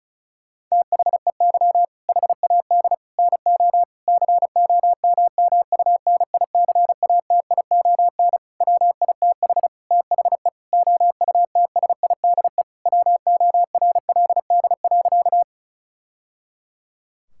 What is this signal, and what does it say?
Morse code, heard as a keyed tone, about 36 wpm, 700 Hz. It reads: THEY HAD NO COMMUNICATION WITH THE OUTSIDE WORLD.
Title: Sentences from Top 1000 Words - Review Part 6 - 35wpm